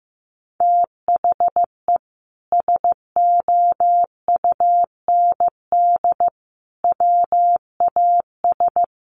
Morse code, heard as a keyed tone, about 15 wpm, 700 Hz. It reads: THE SOUND WAS
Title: Sentences from Top 1200 Words 15wpm